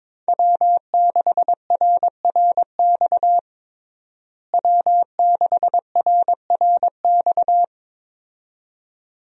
Morse code, heard as a keyed tone, about 22 wpm, 700 Hz. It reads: W6RRX W6RRX
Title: USA Call Signs - 5 Characters - Repeated 2x - 22wpm